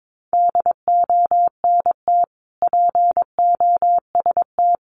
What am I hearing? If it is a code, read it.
Morse code, heard as a keyed tone, about 22 wpm, 700 Hz. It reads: DONT POST
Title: Sentences from Top 1000 Words 22wpm